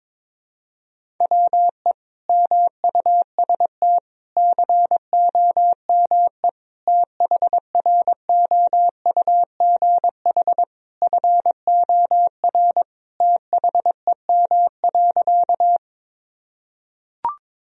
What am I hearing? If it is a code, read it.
Morse code, heard as a keyed tone, about 22 wpm, 700 Hz. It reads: WE MUST COME THROUGH FOR THEM.
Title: Sentences from Top 1000 Words - Review Part 1 - 22wpm